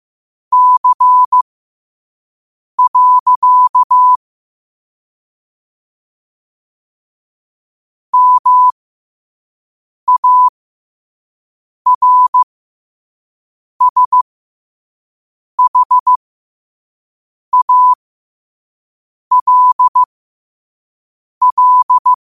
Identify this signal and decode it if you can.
Morse code, heard as a keyed tone, about 15 wpm, spaced as for 5 wpm, 1000 Hz. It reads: C. MARSHALL